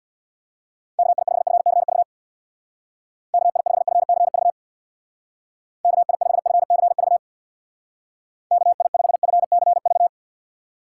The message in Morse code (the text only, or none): KI5FCV KI5FCV KI5FCV KI5FCV